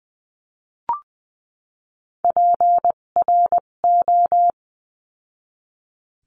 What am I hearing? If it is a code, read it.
Morse code, heard as a keyed tone, about 20 wpm, spaced as for 17 wpm, 700 Hz. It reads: PRO
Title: 257 Lesson 42b - Words 14-20wpm